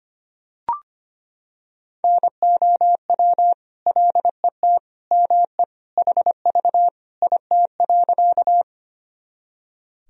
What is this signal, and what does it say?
Morse code, heard as a keyed tone, about 25 wpm, 700 Hz. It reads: NOW LET ME HV IT.
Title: Sentences from Top 300 Words and Abbreviations 25wpm